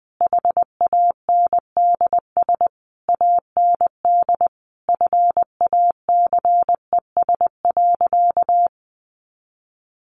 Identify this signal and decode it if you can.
Morse code, heard as a keyed tone, about 20 wpm, 700 Hz. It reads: HANDS AND FACES.
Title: A Sound Of Thunder 20wpm